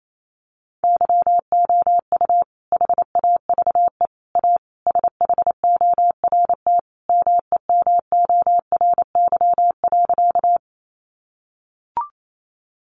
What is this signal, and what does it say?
Morse code, heard as a keyed tone, about 28 wpm, 700 Hz. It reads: YOU HAVE A SHORT MEMORY.
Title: Sentences from Top 1300 Words 28wpm